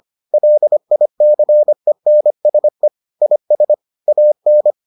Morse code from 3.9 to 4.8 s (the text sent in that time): AN